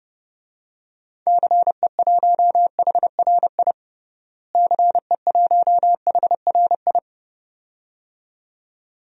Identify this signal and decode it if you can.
Morse code, heard as a keyed tone, about 30 wpm, 700 Hz. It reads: CE1HRI CE1HRI
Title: DXCC-100 Call Signs - 6 Characters - Repeated 2x - 30wpm